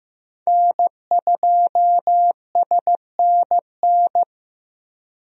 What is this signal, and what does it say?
Morse code, heard as a keyed tone, about 16 wpm, 700 Hz. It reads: N2SNN
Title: USA Call Signs - 5 Characters - Repeated 3x - 15wpm